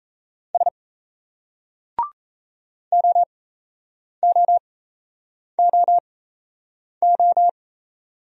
Morse code, heard as a keyed tone, about 37 wpm, spaced as for 8 wpm, 700 Hz. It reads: S OOOO